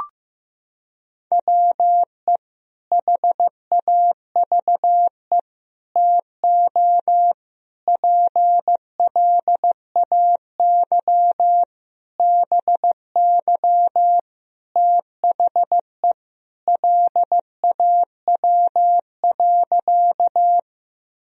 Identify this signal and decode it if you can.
Morse code, heard as a keyed tone, about 16 wpm, 700 Hz. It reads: WE HAVE TO PLAY BY THE LAW.
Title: Sentences from Top 1000 Words - Review Part 3 - 15wpm